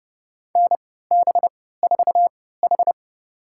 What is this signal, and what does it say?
Morse code, heard as a keyed tone, about 30 wpm, spaced as for 17 wpm, 700 Hz. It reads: NB4H
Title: USA Call Signs - 4 Characters - 3x Character Spacing - 30wpm